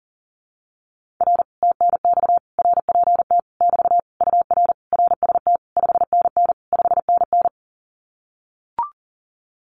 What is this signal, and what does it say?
Morse code, heard as a keyed tone, about 37 wpm, 700 Hz. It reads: R TNX RPT <BT> UR RST 5NN 5NN
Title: Common QSO Phrases Encore 40wpm